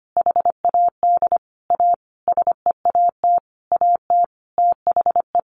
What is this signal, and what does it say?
Morse code, heard as a keyed tone, about 25 wpm, 700 Hz. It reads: HAD A SEAT AT THE
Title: Sentences from Top 1500 Words 25wpm